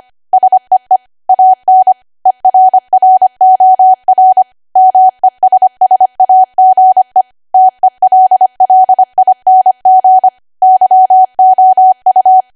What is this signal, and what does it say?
Morse code, heard as a keyed tone, about 25 wpm, 750 Hz. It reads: SEE AN ERROR MESSAGE TELLING YOU